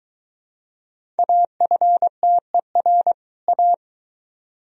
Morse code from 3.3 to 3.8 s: A

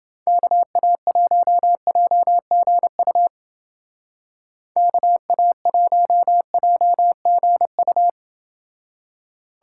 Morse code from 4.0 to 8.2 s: KA1JGU

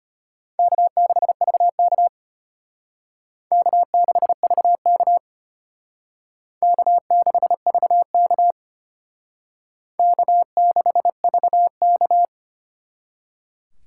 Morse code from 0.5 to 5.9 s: K6VK K6VK

